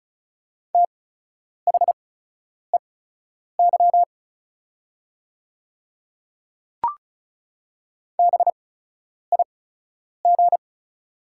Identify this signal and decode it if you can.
Morse code, heard as a keyed tone, about 36 wpm, spaced as for 10 wpm, 700 Hz. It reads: THEY BIG
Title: 8x Character Spacing - Top 300 Words - 35wpm